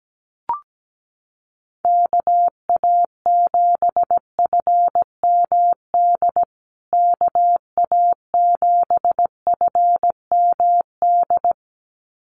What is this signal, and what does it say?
Morse code, heard as a keyed tone, about 17 wpm, 700 Hz. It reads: KA7FMD KA7FMD
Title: USA Call Signs Repeated 2x 17wpm